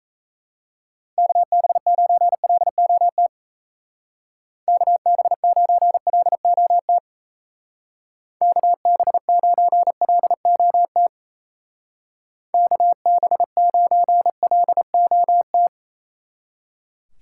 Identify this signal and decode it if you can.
Morse code, heard as a keyed tone, about 35 wpm, 700 Hz. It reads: KB9LOT KB9LOT KB9LOT KB9LOT
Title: Speed Racing - USA Call Signs 28wpm